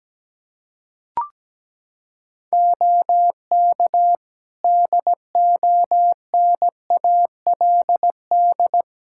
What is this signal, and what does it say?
Morse code, heard as a keyed tone, about 17 wpm, 700 Hz. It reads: OK DONALD
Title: Practice QSO - Exchange 3 - More Info - 17wpm